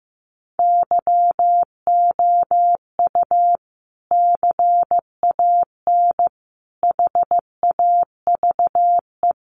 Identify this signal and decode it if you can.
Morse code, heard as a keyed tone, about 15 wpm, 700 Hz. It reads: YOU CAN HAVE
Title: Sentences from Top 100 Words 15wpm